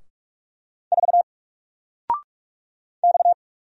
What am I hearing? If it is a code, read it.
Morse code, heard as a keyed tone, about 44 wpm, 700 Hz. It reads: 4 X
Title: Single Letter-Number 45wpm